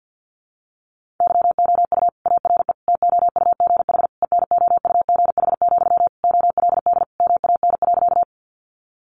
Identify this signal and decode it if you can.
Morse code, heard as a keyed tone, about 50 wpm, 700 Hz. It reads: YOU ARE TOUGH ENOUGH, OLD MAN.
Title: Sentences from Top 1200 Words 50wpm